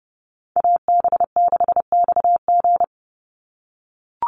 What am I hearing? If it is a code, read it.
Morse code, heard as a keyed tone, about 30 wpm, 700 Hz. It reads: AB6XG
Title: Call Signs - 30wpm